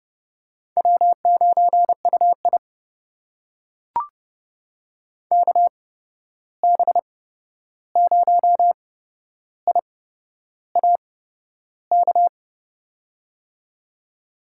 Morse code, heard as a keyed tone, about 30 wpm, spaced as for 11 wpm, 700 Hz. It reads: W9UI KB0IAK
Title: USA Call Signs - 8x Character Spacing - 30wpm